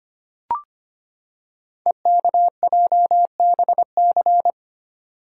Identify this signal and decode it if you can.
Morse code, heard as a keyed tone, about 25 wpm, 700 Hz. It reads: EKJBC
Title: Code Groups - 5 Letters - 25wpm